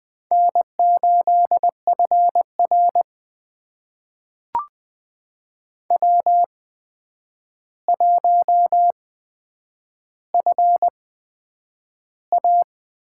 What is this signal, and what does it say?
Morse code, heard as a keyed tone, about 20 wpm, spaced as for 9 wpm, 700 Hz. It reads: N8FR W1FA
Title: USA Call Signs - 4 Characters - 8x Character Spacing - 20wpm